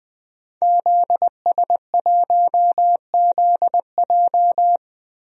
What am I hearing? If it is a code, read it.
Morse code, heard as a keyed tone, about 20 wpm, 700 Hz. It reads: ZS1ZJ
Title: DXCC-100 Call Signs - 5 Characters - Repeated 3x - 20wpm